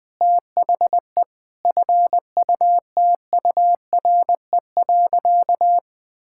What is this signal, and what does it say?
Morse code, heard as a keyed tone, about 21 wpm, 700 Hz. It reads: THE FUTURE.
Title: Sentences with 6-Word Phrases - 20wpm